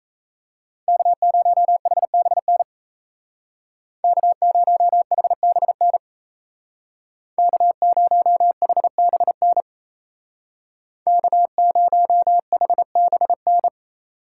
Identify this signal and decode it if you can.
Morse code, heard as a keyed tone, about 33 wpm, 700 Hz. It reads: K0HBN K0HBN K0HBN K0HBN
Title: Speed Racing - USA 1x3 Call Signs 28wpm